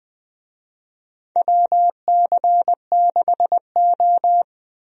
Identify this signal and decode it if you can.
Morse code, heard as a keyed tone, about 20 wpm, 700 Hz. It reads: WC6O
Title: USA Call Signs - 4 Characters - 20wpm